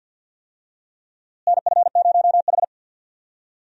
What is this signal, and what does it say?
Morse code, heard as a keyed tone, about 46 wpm, 700 Hz. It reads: NF0H
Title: USA Call Signs - 4 Characters - Repeated 2x - 50wpm